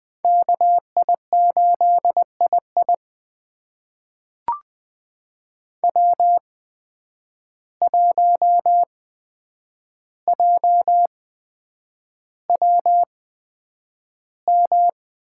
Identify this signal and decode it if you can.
Morse code, heard as a keyed tone, about 20 wpm, spaced as for 9 wpm, 700 Hz. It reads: KI8II W1JWM
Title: USA Call Signs - 8x Character Spacing - 20wpm